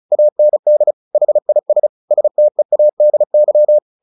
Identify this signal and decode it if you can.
Morse code, heard as a keyed tone, about 34 wpm, 600 Hz. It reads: AND HIS STEADY
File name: the old man and the sea / the old man and the sea 1 35wpm